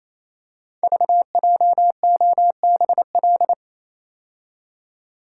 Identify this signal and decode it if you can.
Morse code, heard as a keyed tone, about 27 wpm, 700 Hz. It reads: VJOBL
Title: Code Groups - 5 Letters - 28wpm